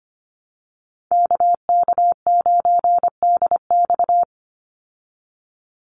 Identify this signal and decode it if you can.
Morse code, heard as a keyed tone, about 25 wpm, 700 Hz. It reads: KK9DX
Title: Call Signs Repeated 2x - 25wpm